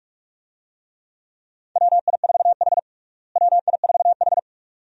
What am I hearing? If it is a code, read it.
Morse code, heard as a keyed tone, about 43 wpm, 700 Hz. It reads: WI4H WI4H
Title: USA Call Signs - 4 Characters - Repeated 2x - 45wpm